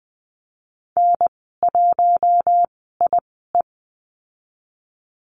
Morse code, heard as a keyed tone, about 20 wpm, spaced as for 15 wpm, 700 Hz. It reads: N1IE
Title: 042 Lesson 8g - Number 1 - Call Signs 20wpm